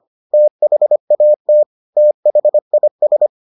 Morse code, heard as a keyed tone, about 25 wpm, 600 Hz. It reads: THAT THIS